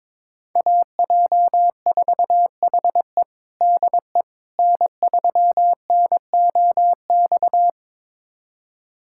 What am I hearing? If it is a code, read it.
Morse code, heard as a keyed tone, about 22 wpm, 700 Hz. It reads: AJ4HE DE N3NOX
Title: USA Callsign DE Callsign - 5 Characters - 22wpm